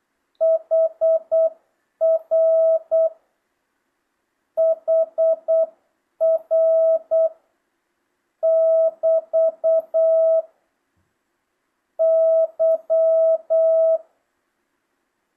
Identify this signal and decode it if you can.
Morse code, heard as a keyed tone, about 8 wpm, 650 Hz. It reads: HR HR <BT> Y
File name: jh8cbh / 2025cw